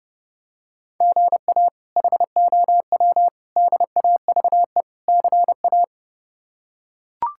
Morse code, heard as a keyed tone, about 32 wpm, 700 Hz. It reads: GA HOW DAVE CA
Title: K1USN SST QSO Practice - 30wpm